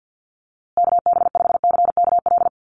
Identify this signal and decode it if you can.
Morse code, heard as a keyed tone, about 49 wpm, 700 Hz. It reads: KB5CKL
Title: USA Call Signs Repeated 3x 50wpm